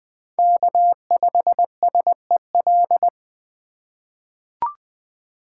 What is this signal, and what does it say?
Morse code, heard as a keyed tone, about 20 wpm, 700 Hz. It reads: K5SEL
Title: USA Call Signs - 5 Characters - 4x Character Spacing - 20wpm